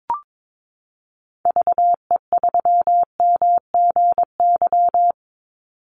Morse code, heard as a keyed tone, about 22 wpm, 700 Hz. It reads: VE3MGY
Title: Call Signs Repeated 3x - 22wpm